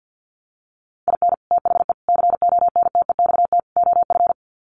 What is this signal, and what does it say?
Morse code, heard as a keyed tone, about 50 wpm, 700 Hz. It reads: IN THE CONTEXT OF
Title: Sets of 4 Words 50wpm